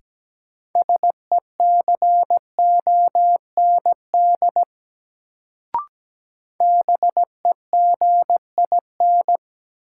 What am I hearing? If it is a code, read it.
Morse code, heard as a keyed tone, about 17 wpm, 700 Hz. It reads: SECOND BEGIN
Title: Sentences from Top 300 Words - New Word Drill - 17wpm